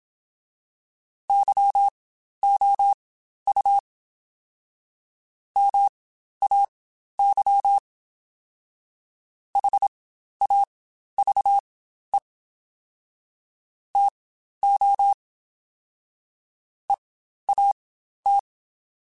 Morse code, heard as a keyed tone, about 27 wpm, spaced as for 12 wpm, 800 Hz. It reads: YOU MAY HAVE TO EAT
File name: Spaced / Chuckles Spaced-mp3s / SoftWords sp @25